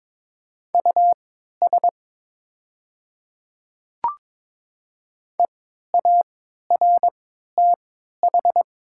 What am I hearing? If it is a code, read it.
Morse code, heard as a keyed tone, about 22 wpm, spaced as for 12 wpm, 700 Hz. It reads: US EARTH
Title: 3x Character Spacing - Top 200 Words - 22wpm